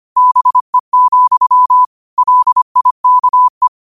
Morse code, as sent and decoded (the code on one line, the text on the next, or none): -... --..-- / .-.. .. -.- .
BE, LIKE